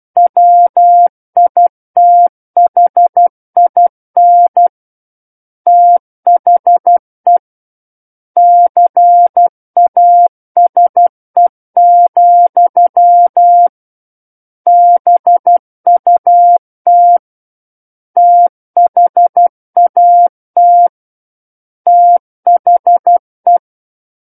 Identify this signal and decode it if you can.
Morse code, heard as a keyed tone, about 12 wpm, 700 Hz. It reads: WITHIN THE CASE, BUT THAT THE
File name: Book One / 03-Chapter 3 - 12 WPM